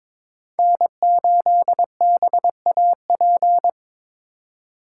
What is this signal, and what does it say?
Morse code, heard as a keyed tone, about 27 wpm, 700 Hz. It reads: N8BAP
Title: Speed Racing - USA 1x3 Call Signs 22wpm